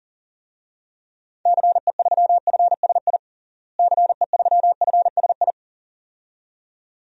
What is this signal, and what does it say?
Morse code, heard as a keyed tone, about 40 wpm, 700 Hz. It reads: CE3FSI CE3FSI
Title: DXCC-100 Call Signs - 6 Characters - Repeated 2x - 40wpm